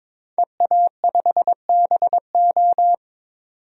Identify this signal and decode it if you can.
Morse code, heard as a keyed tone, about 22 wpm, 700 Hz. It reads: EA5BO
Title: DXCC-100 Call Signs - 5 Characters - 22wpm